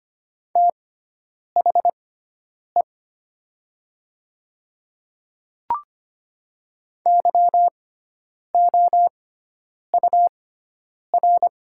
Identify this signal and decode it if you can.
Morse code, heard as a keyed tone, about 25 wpm, spaced as for 9 wpm, 700 Hz. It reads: THE YOUR
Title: 6x Character Spacing - Top 100 Words - 25wpm